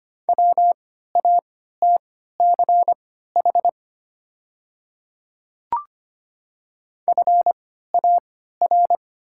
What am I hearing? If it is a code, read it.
Morse code, heard as a keyed tone, about 25 wpm, spaced as for 14 wpm, 700 Hz. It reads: WATCH FAR
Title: 3x Character Spacing - Top 300 Words - 25wpm